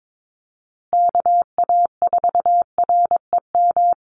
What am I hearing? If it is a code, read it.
Morse code, heard as a keyed tone, about 22 wpm, 700 Hz. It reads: KA4REM
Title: USA Call Signs Repeated 2x 22wpm